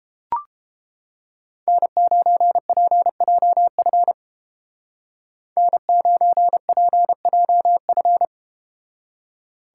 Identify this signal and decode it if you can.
Morse code, heard as a keyed tone, about 31 wpm, 700 Hz. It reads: N9PJF N9PJF